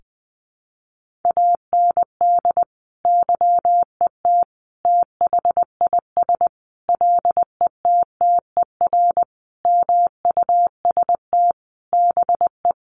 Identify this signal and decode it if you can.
Morse code, heard as a keyed tone, about 20 wpm, 700 Hz. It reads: AND YET THIS LETTER MUST BE
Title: Sentences from Top 1400 Words 20wpm